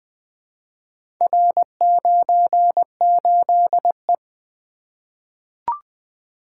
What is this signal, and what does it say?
Morse code, heard as a keyed tone, about 20 wpm, 700 Hz. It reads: R98E
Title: Four Letter-Number Combinations - 20wpm